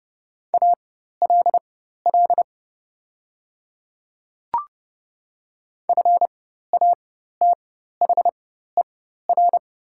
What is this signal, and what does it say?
Morse code, heard as a keyed tone, about 30 wpm, spaced as for 14 wpm, 700 Hz. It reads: ALL FATHER
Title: 4x Character Spacing - Top 300 Words - 30wpm